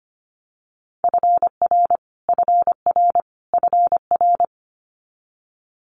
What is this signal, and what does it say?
Morse code, heard as a keyed tone, about 25 wpm, 700 Hz. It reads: FR FR FR
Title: Multi-letter Phonemes Repeated 3x 25wpm